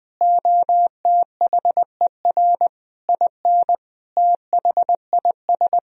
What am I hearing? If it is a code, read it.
Morse code, heard as a keyed tone, about 20 wpm, 700 Hz. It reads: OTHER IN THIS
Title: Sentences from Top 1000 Words - Review Part 1 - 20wpm